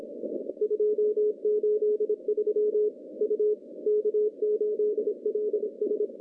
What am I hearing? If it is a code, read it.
Morse code, heard as a keyed tone, about 25 wpm, 410 Hz. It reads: 283 U K8LS
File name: contests / CW-SS-k5tr- -0200